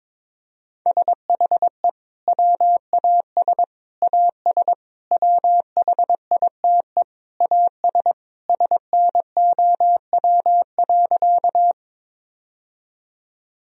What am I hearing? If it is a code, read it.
Morse code, heard as a keyed tone, about 22 wpm, 700 Hz. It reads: SHE WAS AS WHITE AS SNOW.
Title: Sentences from Top 1000 Words - Review Part 3 - 22wpm